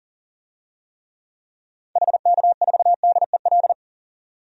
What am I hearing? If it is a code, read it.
Morse code, heard as a keyed tone, about 39 wpm, 700 Hz. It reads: HK4DEL
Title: DXCC-100 Call Signs - 6 Characters - 40wpm